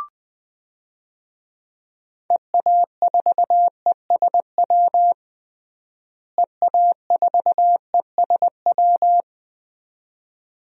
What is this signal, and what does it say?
Morse code, heard as a keyed tone, about 20 wpm, 700 Hz. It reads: EA4ESW EA4ESW
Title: DXCC-100 Call Signs - 6 Characters - Repeated 2x - 20wpm